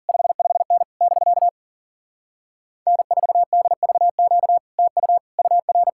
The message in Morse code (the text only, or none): 55N <BK> N4DVQ TU UR